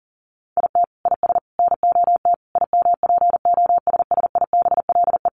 Morse code, heard as a keyed tone, about 40 wpm, 700 Hz. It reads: IT IS NOT IMPOSSIBLE
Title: Sentences from Top 1100 Words 40wpm